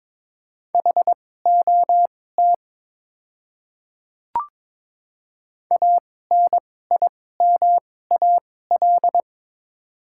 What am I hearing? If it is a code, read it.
Morse code, heard as a keyed tone, about 22 wpm, spaced as for 16 wpm, 700 Hz. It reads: HOT ANIMAL
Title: 2x Character Spacing - Top 200 Words - 22wpm